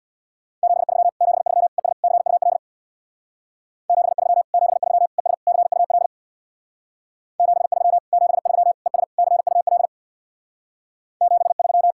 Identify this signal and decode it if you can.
Morse code, heard as a keyed tone, about 64 wpm, 700 Hz. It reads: 73 73 ES CUL 73 73 ES CUL 73 73 ES CUL 73